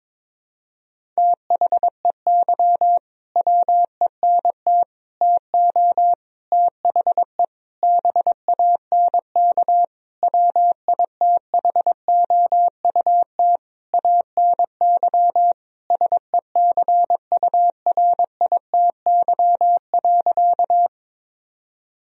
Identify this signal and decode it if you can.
Morse code, heard as a keyed tone, about 22 wpm, 700 Hz. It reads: THEY WENT TO THE BANK WITHOUT ANY SECURITY.
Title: Sentences from Top 1000 Words - Review Part 4 - 22wpm